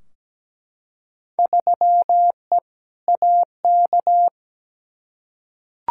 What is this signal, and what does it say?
Morse code, heard as a keyed tone, about 17 wpm, 700 Hz. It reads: 3E AK
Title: Field Day Exchanges Encore - 17wpm